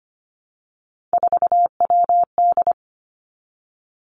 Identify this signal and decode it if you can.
Morse code, heard as a keyed tone, about 25 wpm, 700 Hz. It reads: 4WD
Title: Three Challenging Letter-Number Combinations 25wpm